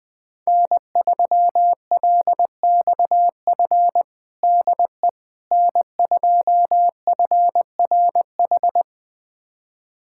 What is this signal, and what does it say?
Morse code, heard as a keyed tone, about 20 wpm, 700 Hz. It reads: N3LXF DE N2FRH
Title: USA Callsign DE Callsign - 5 Characters - 20wpm